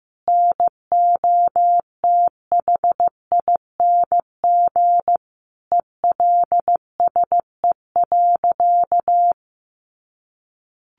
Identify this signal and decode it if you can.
Morse code, heard as a keyed tone, about 15 wpm, 700 Hz. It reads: NOTHING ELSE.
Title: Sentences from Top 800 Words 15wpm